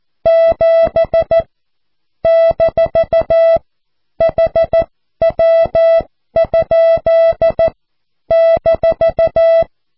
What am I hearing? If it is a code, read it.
Morse code, heard as a keyed tone, about 14 wpm, spaced as for 12 wpm, 660 Hz. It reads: 7 - HW? -
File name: cw / QSO